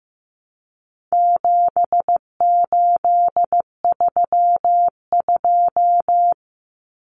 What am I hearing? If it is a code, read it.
Morse code, heard as a keyed tone, about 15 wpm, 700 Hz. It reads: 7832
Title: Common Cut Numbers - 4-Digit 15wpm